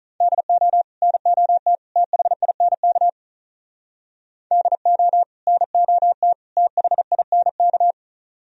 Morse code, heard as a keyed tone, about 46 wpm, 700 Hz. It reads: DO NOT THINK DO NOT THINK